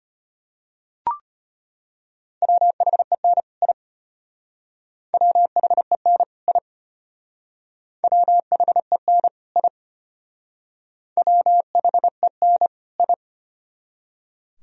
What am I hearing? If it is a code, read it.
Morse code, heard as a keyed tone, about 31 wpm, 700 Hz. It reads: WHEN I WHEN I WHEN I WHEN I
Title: Speed-Racing - Sets of 2 Words 25wpm